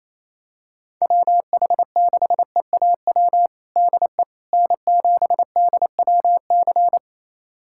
Morse code, heard as a keyed tone, about 28 wpm, 700 Hz. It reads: WH6EAW DE N7DWC
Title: USA Callsign DE Callsign 28wpm